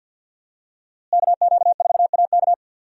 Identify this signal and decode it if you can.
Morse code, heard as a keyed tone, about 52 wpm, 700 Hz. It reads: KQ4AK
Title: USA Call Signs - 5 Characters - Repeated 2x - 50wpm